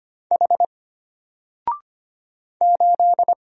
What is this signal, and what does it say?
Morse code, heard as a keyed tone, about 25 wpm, 700 Hz. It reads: H 8
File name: Single-Letter-Number-25wpm